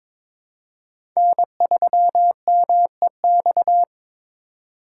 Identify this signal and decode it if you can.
Morse code, heard as a keyed tone, about 22 wpm, 700 Hz. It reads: N3MEX
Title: USA Call Signs - 3x Character Spacing - 22wpm